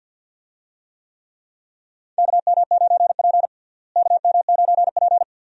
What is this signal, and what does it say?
Morse code, heard as a keyed tone, about 51 wpm, 700 Hz. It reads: KM9P KM9P
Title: USA Call Signs - 4 Characters - Repeated 2x - 50wpm